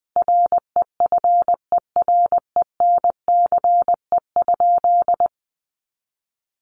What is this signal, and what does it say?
Morse code, heard as a keyed tone, about 20 wpm, 700 Hz. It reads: REFERENCE?
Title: Sentences from Top 1200 Words 20wpm